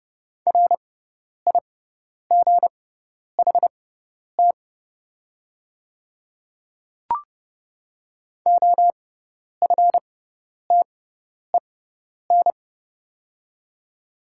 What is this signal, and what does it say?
Morse code, heard as a keyed tone, about 30 wpm, spaced as for 10 wpm, 700 Hz. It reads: RIGHT OFTEN
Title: 6x Character Spacing - Top 500 Words - 30wpm